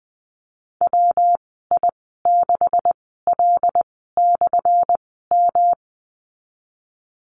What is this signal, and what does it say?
Morse code, heard as a keyed tone, about 19 wpm, spaced as for 14 wpm, 700 Hz. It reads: WI6L/M
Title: 179 Lesson 29g - Slash - Call Signs 20wpm